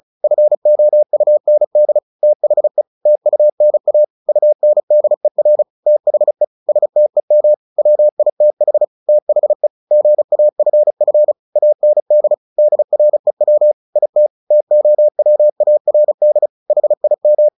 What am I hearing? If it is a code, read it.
Morse code, heard as a keyed tone, about 34 wpm, 600 Hz. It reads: FOUND THE TUNA UNDER THE STEM WITH THE GAFF AND DREW IT TOWARD HIM
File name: the old man and the sea / the old man and the sea 1 35wpm